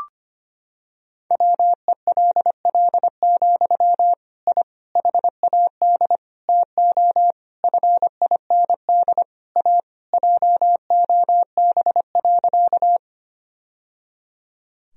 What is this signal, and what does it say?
Morse code, heard as a keyed tone, about 25 wpm, 700 Hz. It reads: WELL, I HAD TO FIND A JOB.
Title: Sentences from Top 1000 Words - Review Part 3 - 25wpm